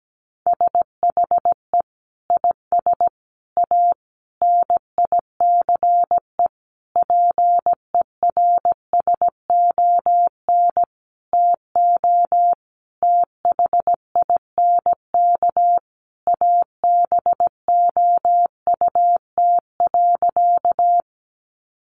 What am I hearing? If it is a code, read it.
Morse code, heard as a keyed tone, about 17 wpm, 700 Hz. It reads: SHE IS A NICE PERSON TO THINK ABOUT.
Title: Sentences from Top 1100 Words 17wpm